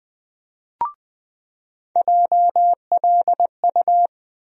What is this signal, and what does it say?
Morse code, heard as a keyed tone, about 20 wpm, 700 Hz. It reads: JLU
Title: USA Call Sign Postfixes - 3 Letters 20wpm